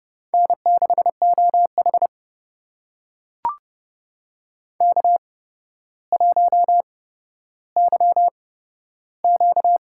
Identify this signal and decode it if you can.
Morse code, heard as a keyed tone, about 30 wpm, spaced as for 13 wpm, 700 Hz. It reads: N6OH K1YQ
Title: USA Call Signs - 4 Characters - 8x Character Spacing - 30wpm